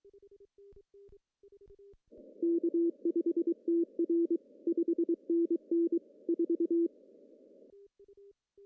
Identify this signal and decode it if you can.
Morse code, heard as a keyed tone, about 23 wpm, 340 Hz. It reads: K5TR 5NN 4
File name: CW / cqww-cw-k5tr